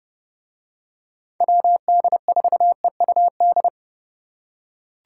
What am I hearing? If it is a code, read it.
Morse code, heard as a keyed tone, about 29 wpm, 700 Hz. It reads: WD4EUD